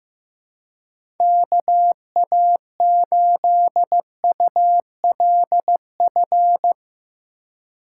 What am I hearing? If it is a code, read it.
Morse code, heard as a keyed tone, about 15 wpm, 700 Hz. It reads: KA8ULF
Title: USA Call Signs - 6 Characters - 3x Character Spacing - 15wpm